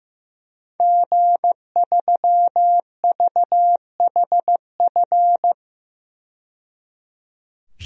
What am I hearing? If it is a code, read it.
Morse code, heard as a keyed tone, about 15 wpm, 700 Hz. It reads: G3VHF
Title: DXCC-100 Call Signs - 5 Characters - 15wpm